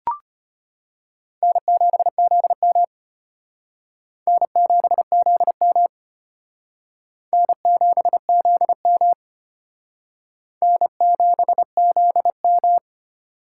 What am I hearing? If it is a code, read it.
Morse code, heard as a keyed tone, about 34 wpm, 700 Hz. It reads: N7ZM N7ZM N7ZM N7ZM